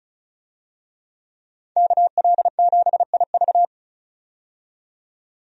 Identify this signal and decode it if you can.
Morse code, heard as a keyed tone, about 36 wpm, 700 Hz. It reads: KL7IV
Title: DXCC-100 Call Signs - 5 Characters - 35wpm